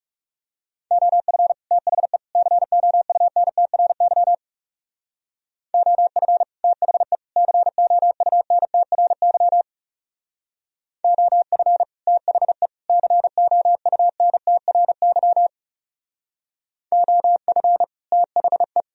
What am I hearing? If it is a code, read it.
Morse code, heard as a keyed tone, about 36 wpm, 700 Hz. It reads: OF THE COUNTRY OF THE COUNTRY OF THE COUNTRY OF THE
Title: Speed-Racing - Sets of 3 Words 30wpm